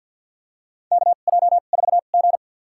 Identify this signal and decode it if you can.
Morse code, heard as a keyed tone, about 47 wpm, spaced as for 35 wpm, 700 Hz. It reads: KJ4G
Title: USA Call Signs - 4 Characters - 2x Character Spacing - 50wpm